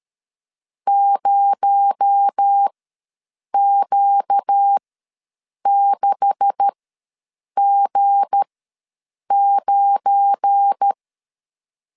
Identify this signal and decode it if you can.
Morse code, heard as a keyed tone, about 13 wpm, spaced as for 7 wpm, 800 Hz. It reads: 0Q6G9
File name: LEZIONI / 9 286 ws 12wpm 3351